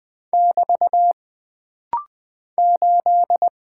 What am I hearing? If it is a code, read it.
Morse code, heard as a keyed tone, about 20 wpm, 700 Hz. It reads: <BT> 8
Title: Single Character (Letter:Number:Punctuation:Prosign) - Rapid-Fire - 20wpm